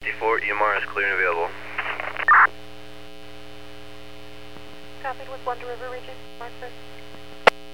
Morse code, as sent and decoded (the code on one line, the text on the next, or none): none